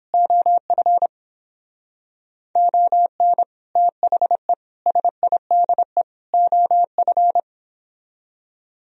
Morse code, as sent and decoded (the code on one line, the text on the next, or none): --- ..-. / --- -. / - .... / ... .. -.. / --- ..-.
OF ON THE SIDE OF